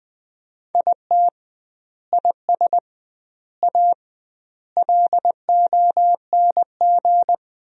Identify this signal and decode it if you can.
Morse code, heard as a keyed tone, about 20 wpm, 700 Hz. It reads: IT IS A LONG
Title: 2x Word-Spacing - Sentences from Top 300 Words - 20wpm